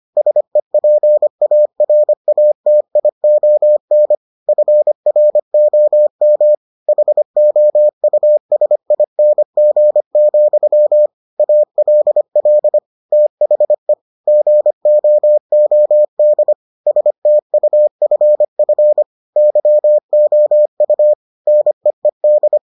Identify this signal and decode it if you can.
Morse code, heard as a keyed tone, about 25 wpm, 600 Hz. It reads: SEPARATION FROM HOUSING, ALL THE GOOD STUFF YOU NEED